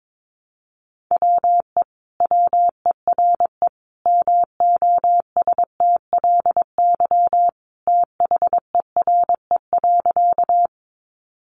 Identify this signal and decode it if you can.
Morse code, heard as a keyed tone, about 22 wpm, 700 Hz. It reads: WE WERE MOSTLY THERE.
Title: Sentences from Top 1300 Words 22wpm